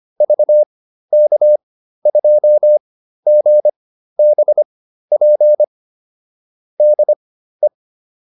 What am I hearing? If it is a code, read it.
Morse code, heard as a keyed tone, about 25 wpm, spaced as for 13 wpm, 600 Hz. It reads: VK2GBP DE
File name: QSO405 13